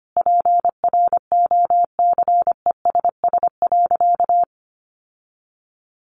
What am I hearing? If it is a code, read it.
Morse code, heard as a keyed tone, about 25 wpm, 700 Hz. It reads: PROCESS.
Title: Sentences from Top 1400 Words 25wpm